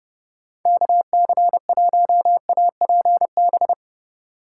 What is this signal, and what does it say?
Morse code, heard as a keyed tone, about 31 wpm, 700 Hz. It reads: KC1APB
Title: USA Call Signs - 2x Character Spacing - 30wpm